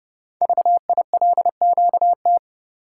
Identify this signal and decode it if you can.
Morse code, heard as a keyed tone, about 30 wpm, 690 Hz. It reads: VILQT
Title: Code Groups - 5 Letters - 30wpm